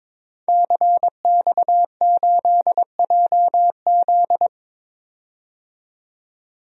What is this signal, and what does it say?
Morse code, heard as a keyed tone, about 21 wpm, 700 Hz. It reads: CX8JZ
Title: DXCC-100 Call Signs - 5 Characters - 22wpm